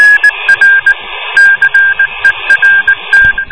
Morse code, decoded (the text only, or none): R CERT